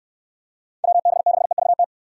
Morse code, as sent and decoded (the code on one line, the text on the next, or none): .-- -... -.... ...- -
WB6EVT